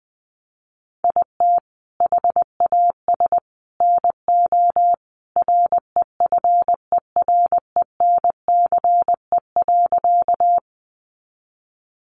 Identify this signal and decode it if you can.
Morse code, heard as a keyed tone, about 20 wpm, 700 Hz. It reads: IT HAS NO REFERENCE.
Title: Sentences from Top 1200 Words 20wpm